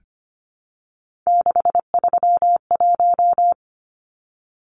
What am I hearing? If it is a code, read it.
Morse code, heard as a keyed tone, about 25 wpm, 700 Hz. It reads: 631
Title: Numbers - 3-Digit 25wpm